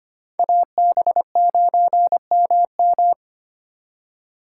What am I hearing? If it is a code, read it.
Morse code, heard as a keyed tone, about 32 wpm, 700 Hz. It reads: AB9MM